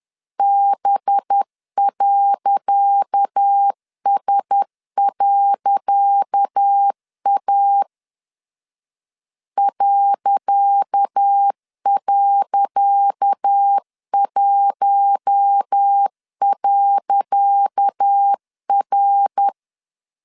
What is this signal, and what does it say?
Morse code, heard as a keyed tone, about 10 wpm, 800 Hz. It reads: B.S.A ..1.R
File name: LEZIONI / 10 SIGNS / 10C DOT 10wpm 2940